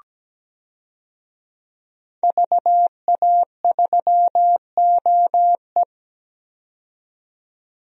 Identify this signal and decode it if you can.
Morse code, heard as a keyed tone, about 17 wpm, 700 Hz. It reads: VA3OE
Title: DXCC-100 Call Signs - 5 Characters - 17wpm